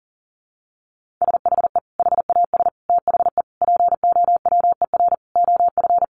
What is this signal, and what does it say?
Morse code, heard as a keyed tone, about 40 wpm, 700 Hz. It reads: SHE HAS THE POWER OF
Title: Sentences from Top 1200 Words 40wpm